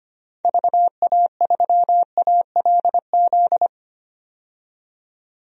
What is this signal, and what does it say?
Morse code, heard as a keyed tone, about 25 wpm, 690 Hz. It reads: VA3ALZ
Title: DXCC-100 Call Signs - 6 Characters - 25wpm